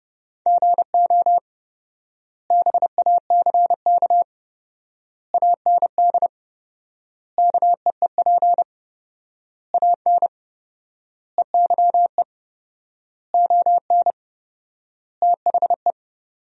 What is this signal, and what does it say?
Morse code, heard as a keyed tone, about 32 wpm, 700 Hz. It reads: GO BACK AND KEEP AN EYE ON THE
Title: 4x Word-Spacing - Sentences from Top 300 Words - 30wpm